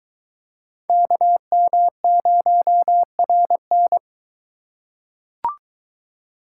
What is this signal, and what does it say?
Morse code, heard as a keyed tone, about 20 wpm, 700 Hz. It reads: KM0RN